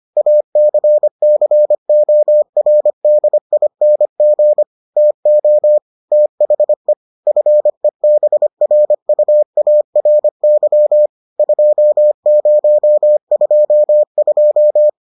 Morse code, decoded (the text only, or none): ACCORDING TO THE FEBRUARY 2022